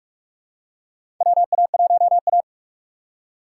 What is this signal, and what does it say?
Morse code, heard as a keyed tone, about 42 wpm, 700 Hz. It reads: WA1A
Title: USA Call Signs - 4 Characters - 2x Character Spacing - 45wpm